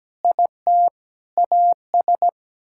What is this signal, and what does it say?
Morse code, heard as a keyed tone, about 17 wpm, 700 Hz. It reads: IT AS
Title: Sentences from Top 1000 Words - Review Part 5 - 17wpm